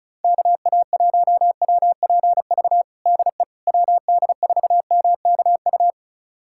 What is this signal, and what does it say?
Morse code, heard as a keyed tone, about 36 wpm, 700 Hz. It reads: KA1WPV DE WD4MKU
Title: USA Callsign DE Callsign - 6 Characters - 35wpm